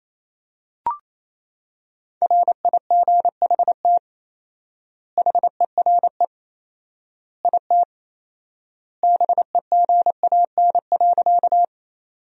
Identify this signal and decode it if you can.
Morse code, heard as a keyed tone, about 28 wpm, 700 Hz. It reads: RIGHT HERE IT BEGAN.
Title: 4x Word-Spacing - Sentences from Top 300 Words - 28wpm